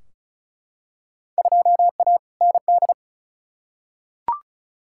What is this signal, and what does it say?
Morse code, heard as a keyed tone, about 34 wpm, 700 Hz. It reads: 2A ND
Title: Field Day Exchanges - 35wpm